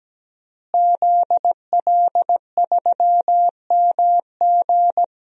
Morse code, heard as a keyed tone, about 17 wpm, 700 Hz. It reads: ZL3MG
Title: DXCC-100 Call Signs - 5 Characters - Repeated 2x - 17wpm